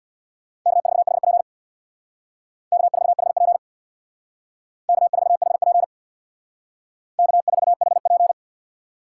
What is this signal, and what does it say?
Morse code, heard as a keyed tone, about 62 wpm, 700 Hz. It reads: K4HP K4HP K4HP K4HP
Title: Speed Racing - USA 1x2 Call Signs 50wpm